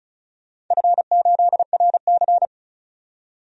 Morse code, heard as a keyed tone, about 34 wpm, 700 Hz. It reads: F8RC
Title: DXCC-100 Call Signs - 4 Characters - 35wpm